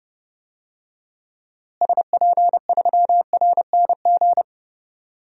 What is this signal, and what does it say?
Morse code, heard as a keyed tone, about 31 wpm, 700 Hz. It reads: SP3RNG
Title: DXCC-100 Call Signs - 6 Characters - Repeated 3x - 30wpm